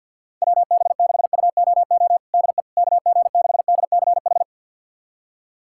Morse code, heard as a keyed tone, about 51 wpm, 700 Hz. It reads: WB6UOO DE KG6DCH
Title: USA Callsign DE Callsign 50wpm